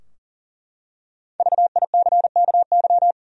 Morse code, heard as a keyed tone, about 40 wpm, 700 Hz. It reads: VICKY
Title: USA Common Names - Max 5 Letters - 40wpm